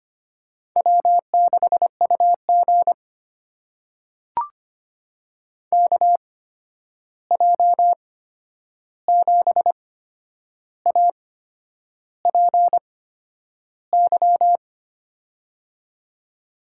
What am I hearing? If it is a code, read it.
Morse code, heard as a keyed tone, about 25 wpm, spaced as for 9 wpm, 700 Hz. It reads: W6UG KJ7APY
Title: USA Call Signs - 8x Character Spacing - 25wpm